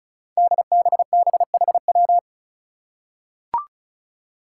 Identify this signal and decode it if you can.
Morse code, heard as a keyed tone, about 36 wpm, 700 Hz. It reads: DBBHW
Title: Five Challenging Letter-Number Combinations 35wpm